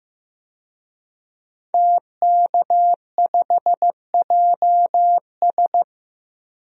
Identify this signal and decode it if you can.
Morse code, heard as a keyed tone, about 15 wpm, 700 Hz. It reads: TK5JS
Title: DXCC-100 Call Signs - 5 Characters - Repeated 3x - 15wpm